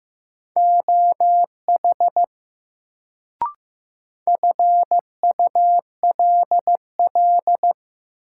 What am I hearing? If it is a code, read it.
Morse code, heard as a keyed tone, about 15 wpm, 700 Hz. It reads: OH FULL
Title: Sentences from Top 500 Words - New Word Drill - 15wpm